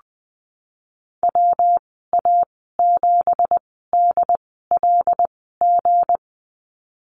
Morse code, heard as a keyed tone, about 21 wpm, spaced as for 15 wpm, 700 Hz. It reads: WA7DLG
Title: 186 Lesson 30g - Letter b - Call Signs 20wpm